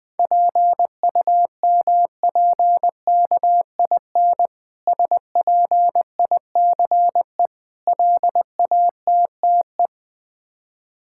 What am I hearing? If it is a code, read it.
Morse code, heard as a keyed tone, about 20 wpm, 690 Hz. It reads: PUMPKIN SPICE LATTE
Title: Difficult Words with Hints - 20wpm